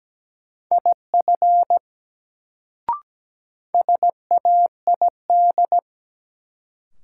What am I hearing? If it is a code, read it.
Morse code, heard as a keyed tone, about 17 wpm, 700 Hz. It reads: IF SAID
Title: Sentences from Top 100 Words - New Word Drill - 17wpm